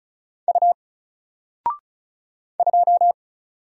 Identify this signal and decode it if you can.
Morse code, heard as a keyed tone, about 35 wpm, 700 Hz. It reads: U 2
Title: Single Character (Letter:Number:Punctuation:Prosign) 35wpm